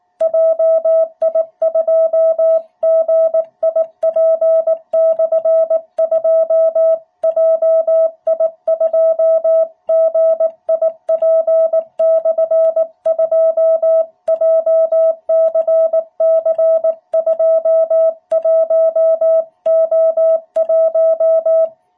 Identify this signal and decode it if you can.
Morse code, heard as a keyed tone, about 18 wpm, 660 Hz. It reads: JI2GIP/2 JI2GIP/2 JCC21O1
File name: jh8cbh / 2021cw